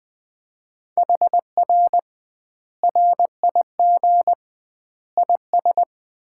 Morse code, heard as a keyed tone, about 20 wpm, 700 Hz. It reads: HR RIG IS
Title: Common QSO Phrases - 2x Word-Spacing - 20wpm